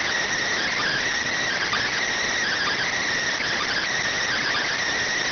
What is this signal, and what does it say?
Emergency Alert System (SAME) tones, heard as harsh buzzing bursts. It is heard from 0.6 to 4.7 s.
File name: findings / MR AXIAL 33,99mm 1.gen-44 rx